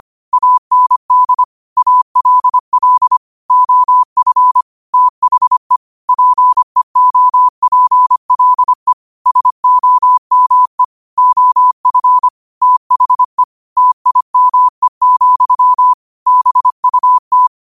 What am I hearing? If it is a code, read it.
Morse code, heard as a keyed tone, about 25 wpm, 1000 Hz. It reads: AND ALL OF THE PEOPLE SOME OF THE TIME, BUT
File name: qotdfiles / 25wpm